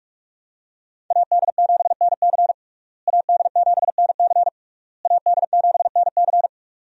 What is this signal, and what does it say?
Morse code, heard as a keyed tone, about 46 wpm, 700 Hz. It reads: AD7NC AD7NC AD7NC
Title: USA Call Signs - 5 Characters - Repeated 3x - 45wpm